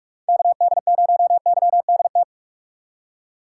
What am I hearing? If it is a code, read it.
Morse code, heard as a keyed tone, about 41 wpm, 700 Hz. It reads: KD0YDT
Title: USA Call Signs - 6 Characters - 6x Character Spacing - 45wpm